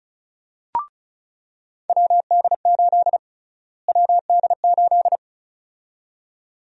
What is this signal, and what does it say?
Morse code, heard as a keyed tone, about 34 wpm, 700 Hz. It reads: WD8 WD8
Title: USA Call Sign Prefixes - Repeated 2x - 35wpm